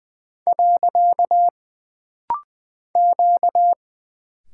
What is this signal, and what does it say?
Morse code, heard as a keyed tone, about 20 wpm, 700 Hz. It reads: . Q
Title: Single Character (Letter:Number:Punctuation:Prosign) - Rapid-Fire - 20wpm